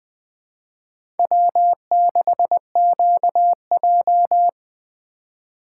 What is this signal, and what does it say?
Morse code, heard as a keyed tone, about 20 wpm, 700 Hz. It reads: W6QJ
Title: USA Call Signs - 4x Character Spacing - 20wpm